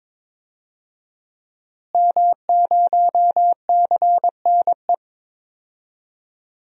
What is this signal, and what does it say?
Morse code, heard as a keyed tone, about 22 wpm, 700 Hz. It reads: M0CNE
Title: DXCC-100 Call Signs - 5 Characters - Repeated 3x - 22wpm